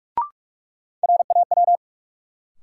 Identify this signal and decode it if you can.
Morse code, heard as a keyed tone, about 46 wpm, 700 Hz. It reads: RAW
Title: Sentences from Top 1500 Words - New Word Drill - 45wpm